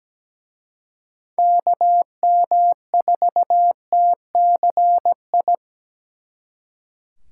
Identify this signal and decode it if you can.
Morse code, heard as a keyed tone, about 20 wpm, 700 Hz. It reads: KM4TCI
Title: Speed Racing - USA 2x3 Call Signs 17wpm